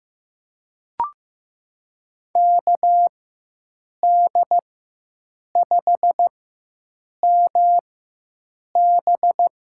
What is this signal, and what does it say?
Morse code, heard as a keyed tone, about 16 wpm, spaced as for 7 wpm, 700 Hz. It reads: KD5MB
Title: USA Call Signs - 4x Character Spacing - 15wpm